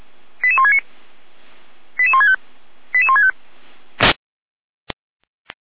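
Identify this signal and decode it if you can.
Five-tone selective call (ZVEI), heard as quick runs of five tones, tones of about 70 ms. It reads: 88169 88165 88165
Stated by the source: ZVEI five-tone selective call received via SDR.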